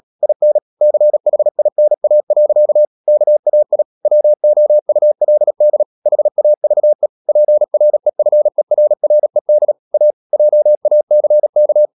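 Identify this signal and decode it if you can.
Morse code, heard as a keyed tone, about 37 wpm, 600 Hz. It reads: IN CHINA. KAI WOULD HAVE PREFERRED A JACK
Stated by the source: Morse code practice at 35–39 wpm sending English text of Ken Follett's novel Never, from CW med Gnister.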